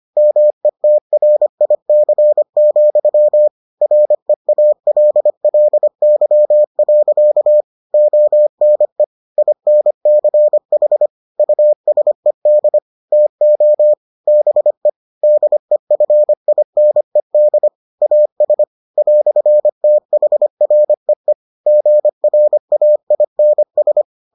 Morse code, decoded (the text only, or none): METRIC, REALLY. ONE INCH USED TO BE DEFINED AS "THREE GRAINS